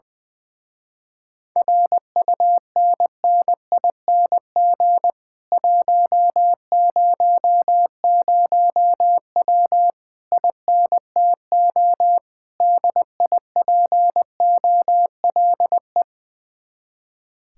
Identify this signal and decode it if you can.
Morse code, heard as a keyed tone, about 19 wpm, 700 Hz. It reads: RUNNING 100W INTO DIPOLE
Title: Speed-Racing Common QSO Phrases - 20wpm